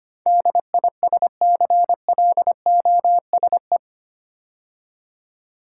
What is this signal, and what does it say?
Morse code, heard as a keyed tone, about 25 wpm, 700 Hz. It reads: DISCLOSE
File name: 8 Letter Words - First 3 Letters Given - 25wpm